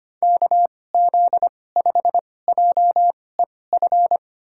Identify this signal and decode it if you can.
Morse code, heard as a keyed tone, about 25 wpm, spaced as for 18 wpm, 700 Hz. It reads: KZ5JEF
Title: USA Call Signs - 6 Characters - 2x Character Spacing - 25wpm